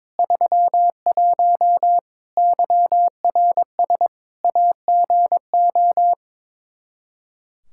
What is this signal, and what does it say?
Morse code, heard as a keyed tone, about 22 wpm, 700 Hz. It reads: 31 YRS AGO
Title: K1USN SST QSO Practice 22wpm